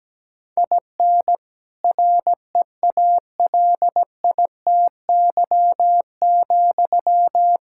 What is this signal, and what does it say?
Morse code, heard as a keyed tone, about 17 wpm, 700 Hz. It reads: IN REALITY,
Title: Sentences from Top 1600 Words 17wpm